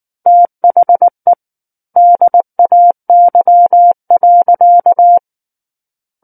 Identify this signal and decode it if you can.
Morse code, heard as a keyed tone, about 19 wpm, 700 Hz. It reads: THE DAY.
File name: Book One / 10-Chapter 10 - 19 WPM